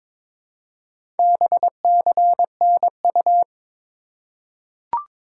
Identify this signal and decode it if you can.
Morse code, heard as a keyed tone, about 22 wpm, 700 Hz. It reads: BCNU
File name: Top 90 QSO Elements - 2x Character Spacing - 22wpm